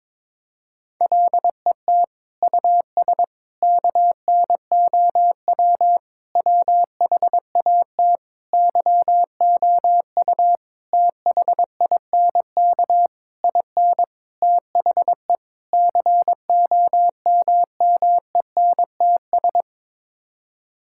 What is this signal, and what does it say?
Morse code, heard as a keyed tone, about 21 wpm, 700 Hz. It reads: LET US KNOW WHAT YOU THINK IN THE COMMENTS
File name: Sets of 9 Words - 22wpm